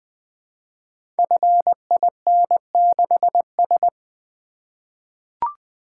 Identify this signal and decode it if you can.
Morse code, heard as a keyed tone, about 20 wpm, 700 Hz. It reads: FIN6S
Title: Code Groups - 5 Letter:Numbers - 20wpm